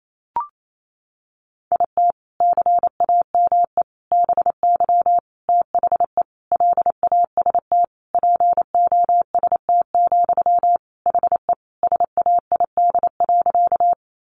Morse code, heard as a keyed tone, about 28 wpm, 700 Hz. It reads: IT CAME BY THE LAST POST, HE SAID.
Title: Sentences from Top 1000 Words 28wpm